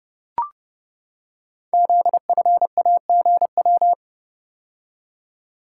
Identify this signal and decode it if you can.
Morse code, heard as a keyed tone, about 30 wpm, 700 Hz. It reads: ZFAGW
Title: Code Groups - 5 Letters - 30wpm